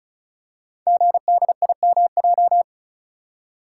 Morse code, heard as a keyed tone, about 36 wpm, 700 Hz. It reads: GDIMJ
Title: Code Groups - 5 Letters - 35wpm